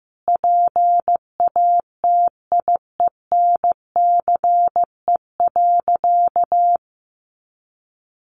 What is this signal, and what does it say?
Morse code, heard as a keyed tone, about 15 wpm, 700 Hz. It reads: PATIENCE.
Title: Sentences from Top 1400 Words 15wpm